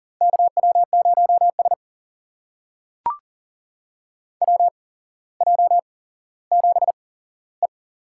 Morse code, heard as a keyed tone, about 40 wpm, spaced as for 17 wpm, 700 Hz. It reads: KW0S WJ7E
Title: USA Call Signs - 8x Character Spacing - 40wpm